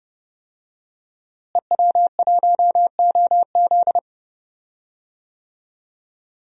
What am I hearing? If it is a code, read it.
Morse code, heard as a keyed tone, about 32 wpm, 700 Hz. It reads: EW1OZ
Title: DXCC-100 Call Signs - 5 Characters - Repeated 3x - 30wpm